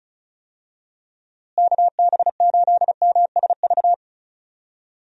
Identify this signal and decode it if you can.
Morse code, heard as a keyed tone, about 36 wpm, 700 Hz. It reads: KB8MSV
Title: USA Call Signs - 6 Characters - Repeated 2x - 35wpm